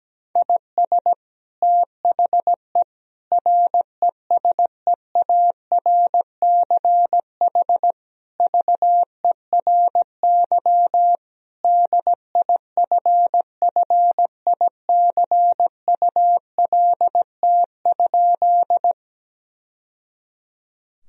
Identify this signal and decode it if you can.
Morse code, heard as a keyed tone, about 17 wpm, 700 Hz. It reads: IS THE RESEARCH VERY DIFFICULT?
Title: Sentences from Top 1000 Words - Review Part 4 - 17wpm